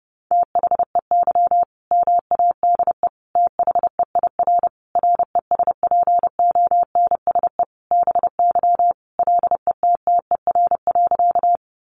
Morse code, heard as a keyed tone, about 30 wpm, 700 Hz. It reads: THEY MADE THEIR RESPONSE BY LETTER.
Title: Sentences from Top 1400 Words 30wpm